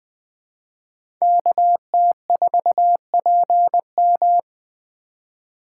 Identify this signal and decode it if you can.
Morse code, heard as a keyed tone, about 21 wpm, 700 Hz. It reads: KT4PM
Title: USA Call Signs - 3x Character Spacing - 20wpm